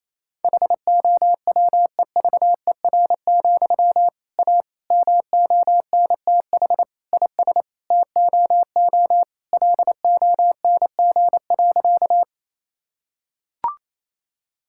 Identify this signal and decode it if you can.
Morse code, heard as a keyed tone, about 28 wpm, 700 Hz. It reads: HOWEVER, A MONTH IS TOO LONG.
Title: Sentences from Top 1000 Words - Review Part 5 - 28wpm